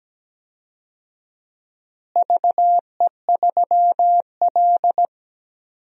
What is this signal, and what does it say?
Morse code, heard as a keyed tone, about 17 wpm, 700 Hz. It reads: VE3L
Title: DXCC-100 Call Signs - 4 Characters - Repeated 3x - 17wpm